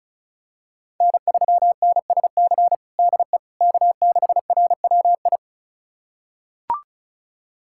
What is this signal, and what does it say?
Morse code, heard as a keyed tone, about 34 wpm, 700 Hz. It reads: N3NSC DE K6RWI
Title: USA Callsign DE Callsign - 5 Characters - 35wpm